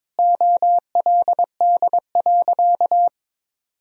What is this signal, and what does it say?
Morse code, heard as a keyed tone, about 21 wpm, 700 Hz. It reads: OLD.
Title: Sentences from Top 1000 Words - Review Part 2 - 22wpm